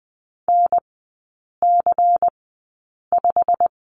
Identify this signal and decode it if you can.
Morse code, heard as a keyed tone, about 20 wpm, spaced as for 8 wpm, 700 Hz. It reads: NC5
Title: 172 Lesson 28e - Number 9 - Three Characters 20wpm